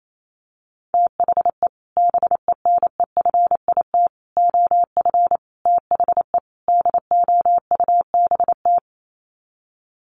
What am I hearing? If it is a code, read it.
Morse code, heard as a keyed tone, about 28 wpm, 690 Hz. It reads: THE BENEFIT OF THE DOUBT
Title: Sets of 5 Words Encore 28wpm